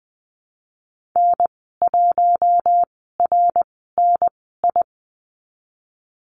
Morse code, heard as a keyed tone, about 20 wpm, spaced as for 14 wpm, 700 Hz. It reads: N1RNI
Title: 069 Lesson 12g - Letter d - Call Signs 20wpm